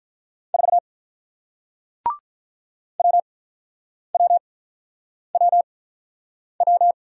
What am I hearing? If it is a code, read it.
Morse code, heard as a keyed tone, about 47 wpm, spaced as for 9 wpm, 700 Hz. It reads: 4 WWWW